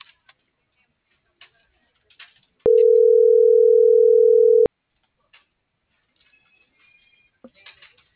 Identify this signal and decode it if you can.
Ringback tone (North America), 440 + 480 Hz.